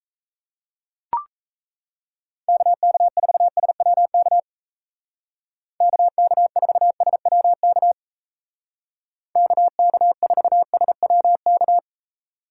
Morse code, heard as a keyed tone, about 35 wpm, 700 Hz. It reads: KK4SWK KK4SWK KK4SWK